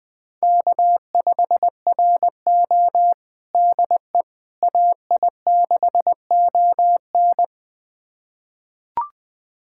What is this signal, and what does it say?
Morse code, heard as a keyed tone, about 20 wpm, 700 Hz. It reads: K5RO DE AI6ON
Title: USA Callsign DE Callsign 20wpm